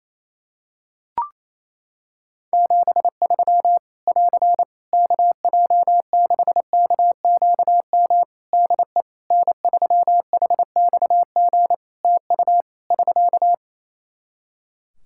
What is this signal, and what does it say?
Morse code, heard as a keyed tone, about 28 wpm, 700 Hz. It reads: 73 <AR> KJ6KQM DE N3HXG TU <SK>
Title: Practice QSO - Exchange 4 - Ending - 28wpm